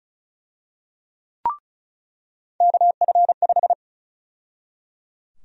Morse code, heard as a keyed tone, about 34 wpm, 700 Hz. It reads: KF5
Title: USA Call Sign Prefixes - 35wpm